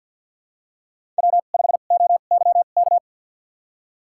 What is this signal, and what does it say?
Morse code, heard as a keyed tone, about 52 wpm, spaced as for 37 wpm, 700 Hz. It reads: W5OYK
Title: USA Call Signs - 5 Characters - 2x Character Spacing - 50wpm